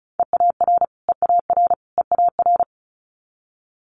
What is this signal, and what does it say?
Morse code, heard as a keyed tone, about 36 wpm, 700 Hz. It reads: EAR EAR EAR
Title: Multi-letter Phonemes Repeated 3x 35wpm